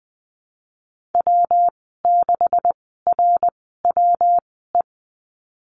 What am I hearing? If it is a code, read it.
Morse code, heard as a keyed tone, about 20 wpm, spaced as for 15 wpm, 700 Hz. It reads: W6RWE
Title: 147 Lesson 24g - Letter y - Call Signs 20wpm